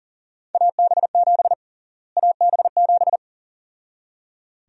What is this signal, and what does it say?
Morse code, heard as a keyed tone, about 40 wpm, 700 Hz. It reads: AB7 AB7
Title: USA Call Sign Prefixes - Repeated 2x - 40wpm